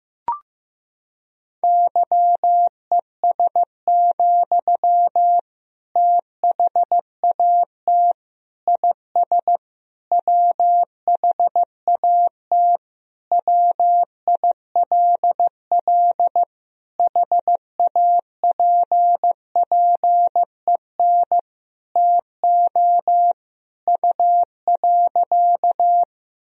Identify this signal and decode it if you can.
Morse code, heard as a keyed tone, about 15 wpm, 700 Hz. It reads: YES, THAT IS WHAT WILL HAPPEN TO U.
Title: Sentences from Top 500 Words and Abbreviations 15wpm